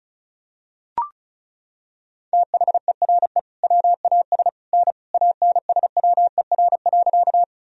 Morse code, heard as a keyed tone, about 36 wpm, 700 Hz. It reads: THERE WAS N ANSWER.
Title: Sentences from Top 300 Words and Abbreviations 35wpm